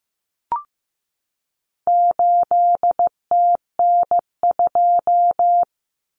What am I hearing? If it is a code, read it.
Morse code, heard as a keyed tone, about 15 wpm, 700 Hz. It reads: 8TN2
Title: Common Cut Numbers - 4-Digit 15wpm